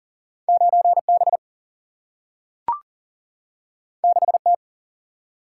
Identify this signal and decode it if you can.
Morse code, heard as a keyed tone, about 40 wpm, 700 Hz. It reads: 9 B 6 T
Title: Single-Letter Single-Number 40wpm